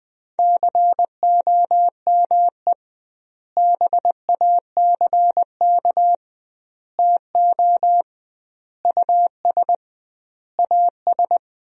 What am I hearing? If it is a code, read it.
Morse code, heard as a keyed tone, about 20 wpm, 700 Hz. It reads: COME BACK TO US AS
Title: 2x Word-Spacing - Sentences from Top 500 Words - 20wpm